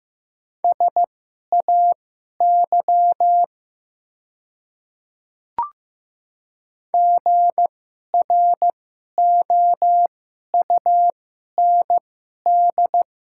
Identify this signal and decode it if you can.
Morse code, heard as a keyed tone, about 15 wpm, spaced as for 11 wpm, 700 Hz. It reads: SAY GROUND
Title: 2x Character Spacing - Top 500 Words - 15wpm